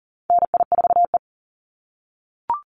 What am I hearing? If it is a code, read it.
Morse code, heard as a keyed tone, about 40 wpm, 700 Hz. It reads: NI4E
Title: Call Signs - 40wpm